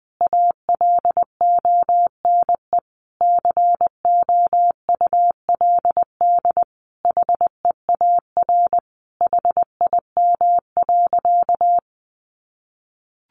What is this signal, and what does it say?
Morse code, heard as a keyed tone, about 20 wpm, 700 Hz. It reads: ALONE COULD HEAR HIM.
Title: Sentences from Top 1000 Words 20wpm